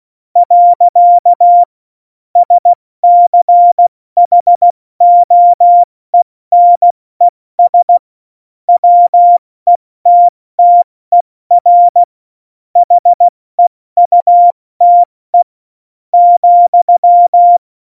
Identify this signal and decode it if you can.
Morse code, heard as a keyed tone, about 16 wpm, spaced as for 14 wpm, 700 Hz. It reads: . SCHOENES WETTER HEUTE ,
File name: Morsekurs / Witze / Witz 14. August 2018, 70 F80